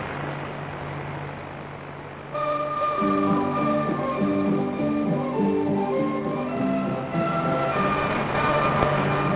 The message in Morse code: none